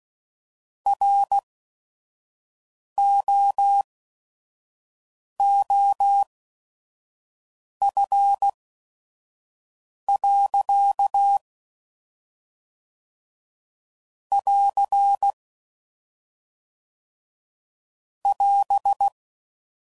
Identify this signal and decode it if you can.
Morse code, heard as a keyed tone, about 16 wpm, spaced as for 5 wpm, 800 Hz. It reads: ROOF. <AR> <AS>
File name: CW Code Groups @5WPM / Weather Sets / Weather 4 28